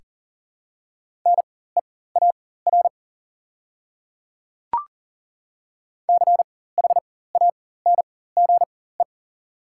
Morse code, heard as a keyed tone, about 40 wpm, spaced as for 19 wpm, 700 Hz. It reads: NEAR CHANGE
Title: 4x Character Spacing - Top 200 Words - 40wpm